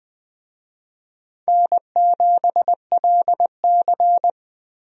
Morse code, heard as a keyed tone, about 20 wpm, 700 Hz. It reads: N7LC
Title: USA Call Signs - 4 Characters - Repeated 2x - 20wpm